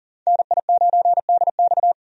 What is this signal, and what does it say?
Morse code, heard as a keyed tone, about 40 wpm, 700 Hz. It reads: NI9DX
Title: USA Call Signs - 5 Characters - 40wpm